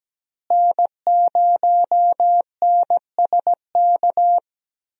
Morse code, heard as a keyed tone, about 17 wpm, 700 Hz. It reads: N0NSK
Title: USA Call Signs - 5 Characters - Repeated 2x - 17wpm